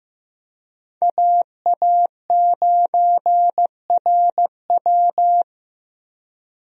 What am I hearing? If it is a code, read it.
Morse code, heard as a keyed tone, about 16 wpm, 700 Hz. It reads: AA9RW
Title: USA Call Signs - 5 Characters - Repeated 3x - 15wpm